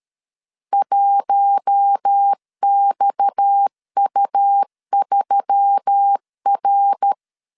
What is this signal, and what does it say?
Morse code, heard as a keyed tone, about 13 wpm, 800 Hz. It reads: 1XU3R
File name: LEZIONI / 6 PKX0 / 6 PKX 12wpm 2024